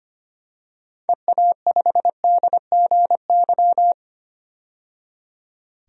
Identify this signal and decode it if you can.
Morse code, heard as a keyed tone, about 25 wpm, 700 Hz. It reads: EA5DGY
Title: DXCC-100 Call Signs - 6 Characters - 25wpm